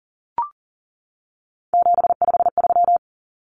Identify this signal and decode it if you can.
Morse code, heard as a keyed tone, about 40 wpm, 700 Hz. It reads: 753
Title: Common Cut Numbers - 3-Digit 40wpm